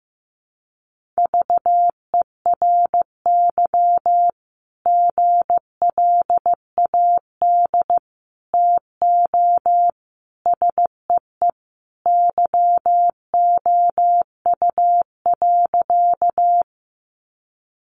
Morse code, heard as a keyed tone, about 15 wpm, 700 Hz. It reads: VERY GLAD TO SEE YOU.
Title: Sentences from Top 800 Words 15wpm